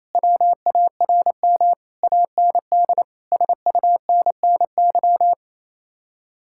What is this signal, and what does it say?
Morse code, heard as a keyed tone, about 36 wpm, 700 Hz. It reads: WARM AND SUNNY